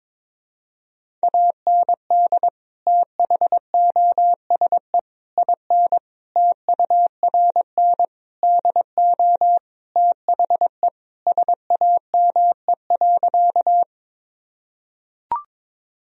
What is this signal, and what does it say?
Morse code, heard as a keyed tone, about 22 wpm, 700 Hz. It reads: AND THOSE IN TURN DO THE SAME.
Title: Sentences from Top 1000 Words - Review Part 1 - 22wpm